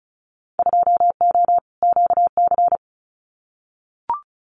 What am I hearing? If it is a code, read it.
Morse code, heard as a keyed tone, about 34 wpm, 700 Hz. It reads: 2O QC
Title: Winter Field Day Exchanges 35wpm